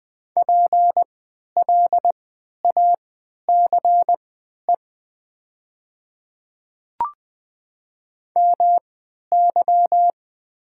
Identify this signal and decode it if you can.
Morse code, heard as a keyed tone, about 20 wpm, spaced as for 11 wpm, 700 Hz. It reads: PLACE MY
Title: 3x Character Spacing - Top 200 Words - 20wpm